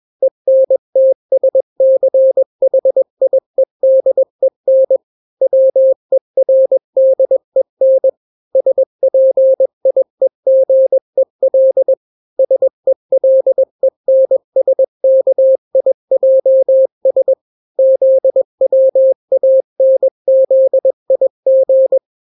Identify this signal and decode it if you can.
Morse code, heard as a keyed tone, about 21 wpm, 530 Hz. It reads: ENTSCHIEDEN WERDEN SPIEGEL SELENSKIJS ZWANZIG
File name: daily / 20wpm